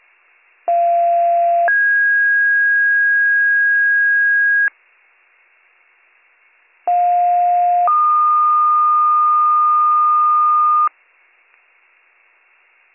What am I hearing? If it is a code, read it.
Two-tone page, heard as a one-second tone, then a three-second tone, about 690 then 1650 Hz, about 710 then 1220 Hz. It is heard from 0.7 to 10.9 s.